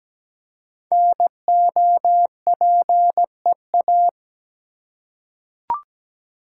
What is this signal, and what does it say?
Morse code, heard as a keyed tone, about 17 wpm, 700 Hz. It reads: NOPEA
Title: Code Groups - 5 Letters - 17wpm